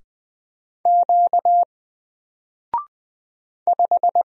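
Morse code, heard as a keyed tone, about 20 wpm, 700 Hz. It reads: Q 5
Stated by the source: Single letter and number Morse drill at 20 wpm, from N4LCD.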